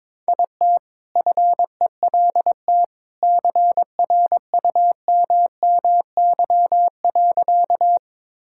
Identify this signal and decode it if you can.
Morse code, heard as a keyed tone, about 22 wpm, 700 Hz. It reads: IT FELT CRUMMY.
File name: All-Age Appropriate Jokes - 22wpm